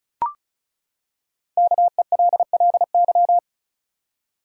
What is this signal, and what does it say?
Morse code, heard as a keyed tone, about 36 wpm, 700 Hz. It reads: KELLY
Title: USA Common Names - Max 5 Letters - 35wpm